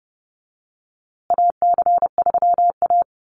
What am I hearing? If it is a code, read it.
Morse code, heard as a keyed tone, about 29 wpm, 700 Hz. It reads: AC3A
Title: USA Call Signs Repeated 3x 30wpm